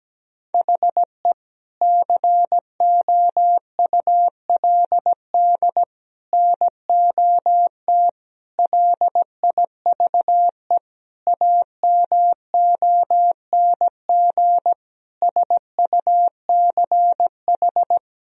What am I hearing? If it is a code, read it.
Morse code, heard as a keyed tone, about 17 wpm, 700 Hz. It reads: HE COULD NOT LIVE AMONG SUCH
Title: Sentences from Top 1000 Words - Review Part 3 - 17wpm